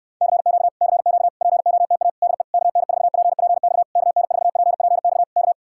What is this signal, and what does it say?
Morse code, heard as a keyed tone, about 69 wpm, 700 Hz. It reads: CQ CQ POTA DE KN4PPX KN4PPX K